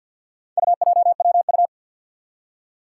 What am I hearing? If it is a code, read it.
Morse code, heard as a keyed tone, about 46 wpm, 700 Hz. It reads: UJWU
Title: Four Challenging Letter-Number Combinations 50wpm